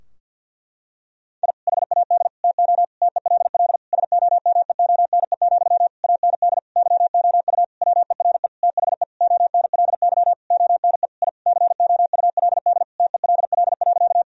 Morse code, heard as a keyed tone, about 52 wpm, 700 Hz. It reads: I HAD TO TELL SOMEONE, AND YOU WERE THE ONLY ONE I COULD TELL.
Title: Sentences from Top 1000 Words - Review Part 4 - 50wpm